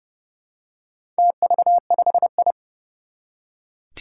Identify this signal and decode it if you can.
Morse code, heard as a keyed tone, about 31 wpm, 700 Hz. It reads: TV5I
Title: Four Challenging Letter-Number Combinations 30wpm